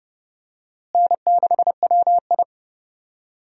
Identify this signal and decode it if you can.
Morse code, heard as a keyed tone, about 29 wpm, 700 Hz. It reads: N6WI
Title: USA Call Signs - 4 Characters - Repeated 3x - 30wpm